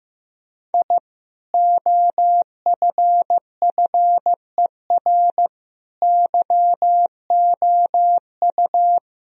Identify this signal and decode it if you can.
Morse code, heard as a keyed tone, about 15 wpm, 700 Hz. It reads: I OFFER YOU
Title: Sentences from Top 1000 Words - Review Part 4 - 15wpm